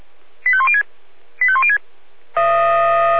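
Five-tone selective call (ZVEI), heard as quick runs of five tones, tones of about 70 ms. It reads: 85106 85106
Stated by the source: ZVEI five-tone selective call received on an SDR.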